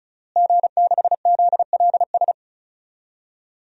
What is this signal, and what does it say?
Morse code, heard as a keyed tone, about 36 wpm, 690 Hz. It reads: G6ZLS
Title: DXCC-100 Call Signs - 5 Characters - Repeated 3x - 35wpm